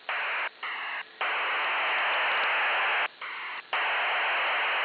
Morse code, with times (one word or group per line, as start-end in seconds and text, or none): none